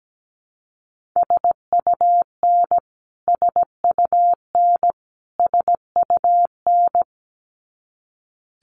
Morse code, heard as a keyed tone, about 17 wpm, 700 Hz. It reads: SUN SUN SUN
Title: Top 500 Words Repeated 17wpm